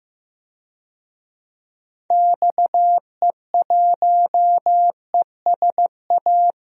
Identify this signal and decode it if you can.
Morse code, heard as a keyed tone, about 15 wpm, 700 Hz. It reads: XE1ESA
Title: DXCC-100 Call Signs - 6 Characters - Repeated 2x - 15wpm